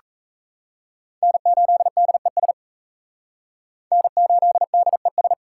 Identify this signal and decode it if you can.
Morse code, heard as a keyed tone, about 34 wpm, 700 Hz. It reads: N8DES N8DES
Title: Speed Racing - USA 1x3 Call Signs 28wpm